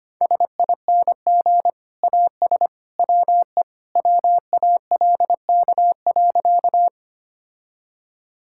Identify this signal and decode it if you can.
Morse code, heard as a keyed tone, about 25 wpm, 700 Hz. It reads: SING AS WE WALK.
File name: practice / 25wpm / Sentences with 2-Word Phrases - 25wpm